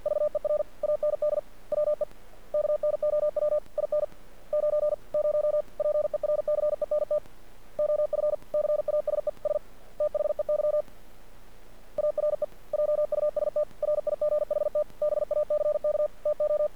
Morse code, heard as a keyed tone, about 50 wpm, 600 Hz. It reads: VER AND WE KNOW IN 90 PERCENT OF CASES THEY ARE JUST RIGHT BACK TO
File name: archived / CRSnet